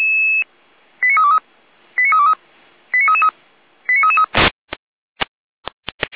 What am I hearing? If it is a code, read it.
Five-tone selective call (ZVEI), heard as quick runs of five tones, tones of about 80 ms. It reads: 89323 89323 89333 89333
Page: /455/SDR/ZVEI/89322/2025.07.29/